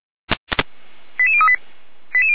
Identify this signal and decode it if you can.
Five-tone selective call (ZVEI), heard as quick runs of five tones, tones of about 70 ms. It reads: 88038
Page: 455/SDR/ZVEI/88207